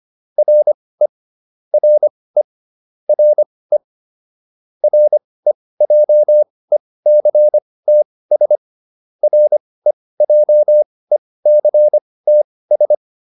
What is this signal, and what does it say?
Morse code, heard as a keyed tone, about 25 wpm, spaced as for 18 wpm, 600 Hz. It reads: RE RE RE REJECTS REJECTS